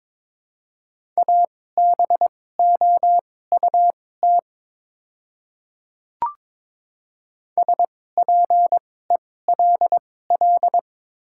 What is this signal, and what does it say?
Morse code, heard as a keyed tone, about 22 wpm, spaced as for 16 wpm, 700 Hz. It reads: ABOUT SPELL
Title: 2x Character Spacing - Top 200 Words - 22wpm